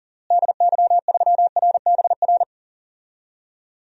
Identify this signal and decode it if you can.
Morse code, heard as a keyed tone, about 39 wpm, 700 Hz. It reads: DY3RBR
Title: DXCC-100 Call Signs - 6 Characters - Repeated 2x - 40wpm